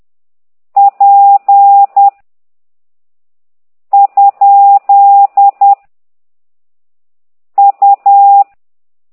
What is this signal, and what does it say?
Morse code, heard as a keyed tone, about 10 wpm, 800 Hz. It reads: P ? U
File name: ct2idw / exerc9